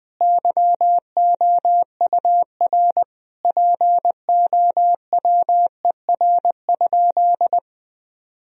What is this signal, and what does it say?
Morse code, heard as a keyed tone, about 20 wpm, 700 Hz. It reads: YOUR POWER?
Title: Sentences from Top 1000 Words - Review Part 2 - 20wpm